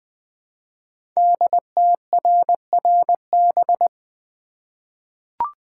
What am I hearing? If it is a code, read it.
Morse code, heard as a keyed tone, about 20 wpm, 700 Hz. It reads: DTRRB
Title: Code Groups - 5 Letters - 20wpm